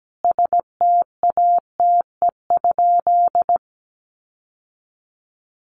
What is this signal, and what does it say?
Morse code, heard as a keyed tone, about 17 wpm, 700 Hz. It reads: STATE?
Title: Sentences from Top 1200 Words 17wpm